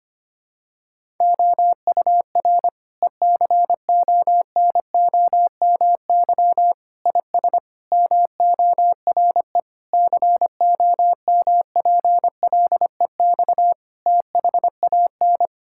Morse code, heard as a keyed tone, about 25 wpm, 700 Hz. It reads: OUR ECONOMY IS MORE COMPLEX THAN
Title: Sentences from Top 1000 Words - Review Part 6 - 25wpm